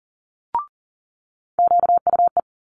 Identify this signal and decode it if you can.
Morse code, heard as a keyed tone, about 38 wpm, 700 Hz. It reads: QUE
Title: Multi-letter Phonemes Quick Fire 40wpm